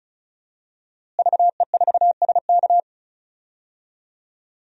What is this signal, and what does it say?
Morse code, heard as a keyed tone, about 35 wpm, 700 Hz. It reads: VE4SK
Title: DXCC-100 Call Signs - 5 Characters - Repeated 2x - 35wpm